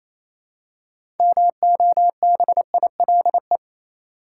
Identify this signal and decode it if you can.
Morse code, heard as a keyed tone, about 28 wpm, 700 Hz. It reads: MOBILE
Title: Six Letter Words - 28wpm